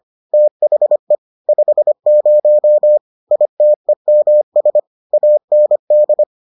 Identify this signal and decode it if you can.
Morse code, heard as a keyed tone, about 25 wpm, 600 Hz. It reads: THE 50 ITEMS AND